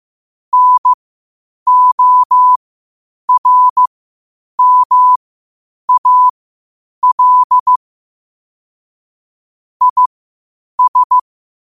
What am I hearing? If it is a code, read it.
Morse code, heard as a keyed tone, about 15 wpm, spaced as for 8 wpm, 1000 Hz. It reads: NORMAL IS
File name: qotdfiles / 8wpm